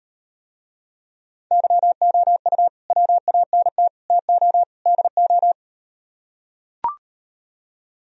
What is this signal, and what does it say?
Morse code, heard as a keyed tone, about 38 wpm, 700 Hz. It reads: YOU WANT TO DO